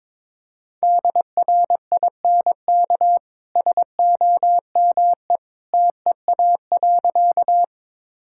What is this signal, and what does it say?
Morse code, heard as a keyed tone, about 21 wpm, 700 Hz. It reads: DRINK SOME TEA.
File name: practice / 22wpm / Easy Sentences with Hints - 22wpm